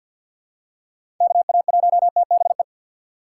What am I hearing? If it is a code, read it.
Morse code, heard as a keyed tone, about 48 wpm, 700 Hz. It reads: KA1TBE
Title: USA Call Signs - 6 Characters - 3x Character Spacing - 50wpm